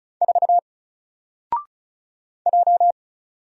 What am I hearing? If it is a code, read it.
Morse code, heard as a keyed tone, about 34 wpm, 700 Hz. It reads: 4 J
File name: Single-Letter-Number-35wpm